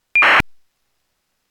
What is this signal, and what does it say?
ACARS aircraft data.